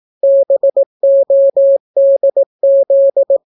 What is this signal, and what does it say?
Morse code, heard as a keyed tone, about 18 wpm, 550 Hz. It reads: BODZ